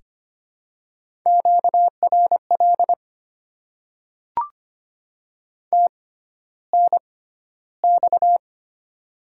Morse code, heard as a keyed tone, about 25 wpm, spaced as for 13 wpm, 700 Hz. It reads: QRL TNX
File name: Top 20 QSO Elements - 6x Character Spacing - 25wpm